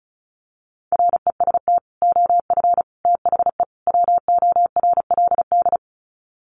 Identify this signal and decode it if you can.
Morse code, heard as a keyed tone, about 34 wpm, 700 Hz. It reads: REST OF THE WORLD
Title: Sets of 4 Words 35wpm